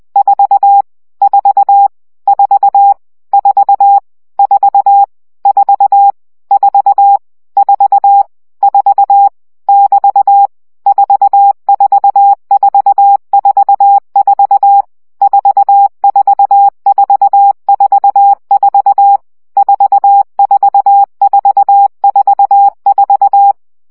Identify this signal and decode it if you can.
Morse code, heard as a keyed tone, about 20 wpm, 800 Hz. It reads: 4 4 4 4 4 4 4 4 4 <BT> 44444 44444 44444
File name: CW gyakorlas 4 17 baud